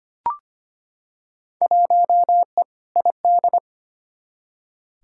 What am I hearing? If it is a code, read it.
Morse code, heard as a keyed tone, about 25 wpm, 700 Hz. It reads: 1E ID
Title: Field Day Exchanges - 25wpm